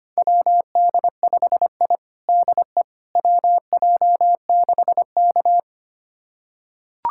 Morse code, heard as a keyed tone, about 25 wpm, 700 Hz. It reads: WD5I DE WJ6K
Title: USA Callsign DE Callsign - 4 Characters - 25wpm